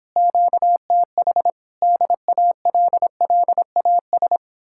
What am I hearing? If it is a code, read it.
Morse code, heard as a keyed tone, about 29 wpm, 700 Hz. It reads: QTH DALLAS